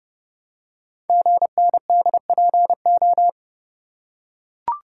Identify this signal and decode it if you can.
Morse code, heard as a keyed tone, about 30 wpm, 700 Hz. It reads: GNDPO
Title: Code Groups - 5 Letters - 30wpm